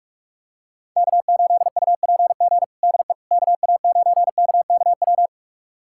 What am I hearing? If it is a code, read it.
Morse code, heard as a keyed tone, about 44 wpm, 700 Hz. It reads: K8UPG DE KA9KKW
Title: USA Callsign DE Callsign 45wpm